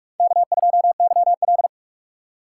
Morse code, heard as a keyed tone, about 43 wpm, 700 Hz. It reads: K2YL
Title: USA Call Signs - 4x Character Spacing - 45wpm